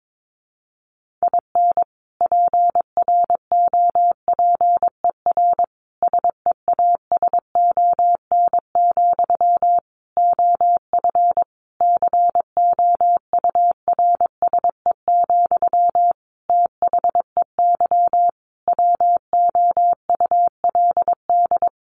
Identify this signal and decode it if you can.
Morse code, heard as a keyed tone, about 22 wpm, 700 Hz. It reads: IN PROPER SEASON, OF COURSE, THEY WOULD
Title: Sentences from Top 1400 Words 22wpm